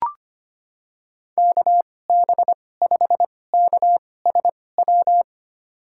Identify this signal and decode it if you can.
Morse code, heard as a keyed tone, about 25 wpm, spaced as for 18 wpm, 700 Hz. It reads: KB5KSW
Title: USA Call Signs - 2x Character Spacing - 25wpm